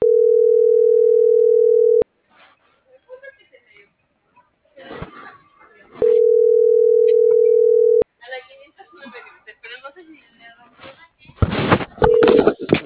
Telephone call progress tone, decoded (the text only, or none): ringback tone